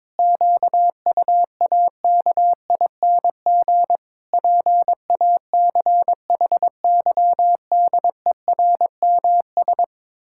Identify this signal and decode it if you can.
Morse code, heard as a keyed tone, about 22 wpm, 700 Hz. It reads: QUAKING PACHYDERMS
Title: Pangram Challenge v2 - 22wpm